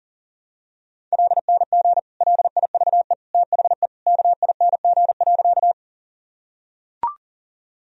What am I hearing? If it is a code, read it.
Morse code, heard as a keyed tone, about 40 wpm, 700 Hz. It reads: LNG LIVE THE KING.
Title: Sentences from Top 400 Words and Abbreviations 40wpm